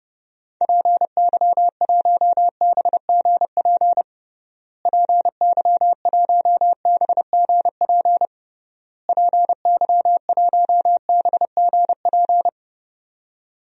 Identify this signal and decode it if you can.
Morse code, heard as a keyed tone, about 30 wpm, 700 Hz. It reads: PY1BGP PY1BGP PY1BGP
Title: DXCC-100 Call Signs - 6 Characters - Repeated 3x - 30wpm